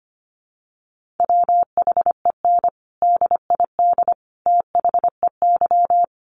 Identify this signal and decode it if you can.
Morse code, heard as a keyed tone, about 25 wpm, 700 Hz. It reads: WHEN DID THEY
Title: Sentences from Top 900 Words 25wpm